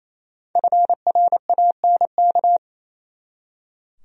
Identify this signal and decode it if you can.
Morse code, heard as a keyed tone, about 28 wpm, 700 Hz. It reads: FRANK
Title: USA Common Names - Max 5 Letters - 28wpm